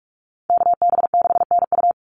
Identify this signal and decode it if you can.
Morse code, heard as a keyed tone, about 44 wpm, 700 Hz. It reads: KB6NU
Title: Call Signs Repeated 3x - 45wpm